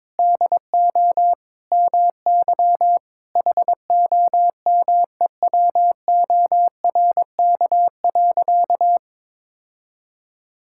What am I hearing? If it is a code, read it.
Morse code, heard as a keyed tone, about 22 wpm, 700 Hz. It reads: DO MY HOMEWORK.
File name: practice / 22wpm / Sentences with 3-Word Phrases - 22wpm